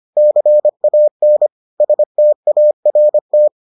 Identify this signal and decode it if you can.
Morse code, heard as a keyed tone, about 25 wpm, 600 Hz. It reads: CAN START